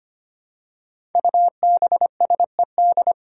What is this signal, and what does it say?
Morse code, heard as a keyed tone, about 25 wpm, 700 Hz. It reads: UBSED
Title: Five Challenging Letter-Number Combinations 25wpm